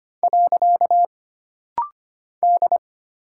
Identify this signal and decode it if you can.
Morse code, heard as a keyed tone, about 25 wpm, 700 Hz. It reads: . D
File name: Single Character (Letter:Number:Punctuation:Prosign) - Rapid-Fire - 25wpm